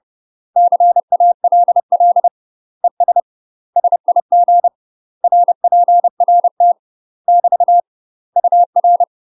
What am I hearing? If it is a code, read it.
Morse code, heard as a keyed tone, about 30 wpm, 700 Hz. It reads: CALL ES SIG RPRT <BT> UR
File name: UK MASTER QSOs TRAINING / 30WPM / UK MASTER QSOs 0044